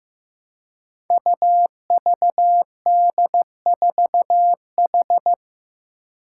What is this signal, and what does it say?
Morse code, heard as a keyed tone, about 15 wpm, 700 Hz. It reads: UVD4H
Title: Five Challenging Letter-Number Combinations 15wpm